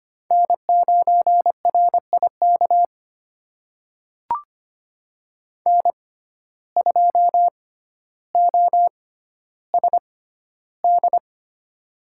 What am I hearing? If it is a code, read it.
Morse code, heard as a keyed tone, about 25 wpm, spaced as for 13 wpm, 700 Hz. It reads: N9RIK N2OSD
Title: USA Call Signs - 5 Characters - 6x Character Spacing - 25wpm